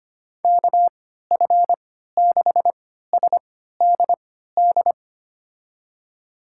Morse code, heard as a keyed tone, about 25 wpm, spaced as for 14 wpm, 700 Hz. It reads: KF6SDD
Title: USA Call Signs - 6 Characters - 3x Character Spacing - 25wpm